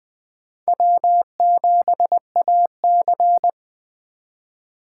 Morse code, heard as a keyed tone, about 20 wpm, 700 Hz. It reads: W7AC
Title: Speed Racing - USA 1x2 Call Signs 17wpm